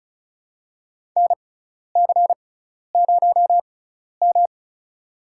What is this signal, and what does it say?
Morse code, heard as a keyed tone, about 36 wpm, spaced as for 12 wpm, 700 Hz. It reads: NC0M
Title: USA Call Signs - 4 Characters - 6x Character Spacing - 35wpm